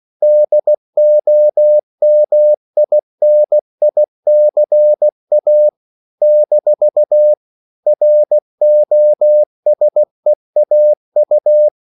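Morse code, heard as a keyed tone, about 16 wpm, 600 Hz. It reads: DOMINICA - ROSEAU
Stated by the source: CW practice of national capitals at 16 words per minute, from ON6ZQ.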